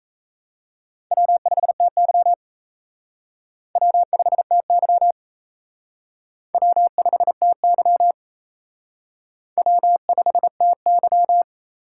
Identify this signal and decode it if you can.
Morse code, heard as a keyed tone, about 34 wpm, 700 Hz. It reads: W5TY W5TY W5TY W5TY
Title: Speed Racing - USA 1x2 Call Signs 28wpm